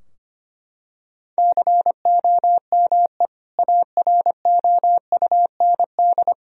Morse code, heard as a keyed tone, about 25 wpm, 700 Hz. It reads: COME AROUND
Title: Sentences from Top 1000 Words - Review Part 3 - 25wpm